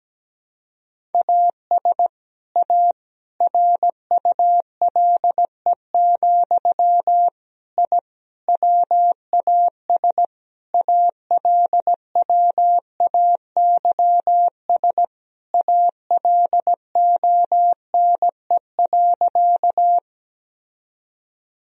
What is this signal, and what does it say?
Morse code, heard as a keyed tone, about 17 wpm, 700 Hz. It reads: AS A RULE, I WAS ALWAYS ALONE.
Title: Sentences from Top 1000 Words - Review Part 2 - 17wpm